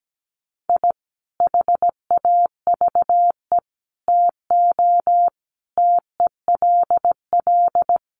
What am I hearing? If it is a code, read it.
Morse code, heard as a keyed tone, about 17 wpm, 700 Hz. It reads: I HAVE TO TELL
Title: Sets of 5 Words 17wpm